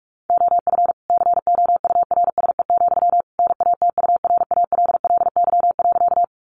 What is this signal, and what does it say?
Morse code, heard as a keyed tone, about 44 wpm, 700 Hz. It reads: OF COURSE, NATURALLY.
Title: Sentences from Top 1300 Words 45wpm